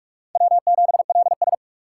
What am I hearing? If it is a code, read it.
Morse code, heard as a keyed tone, about 48 wpm, 700 Hz. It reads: W7LS
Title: USA Call Signs - 4 Characters - 8x Character Spacing - 45wpm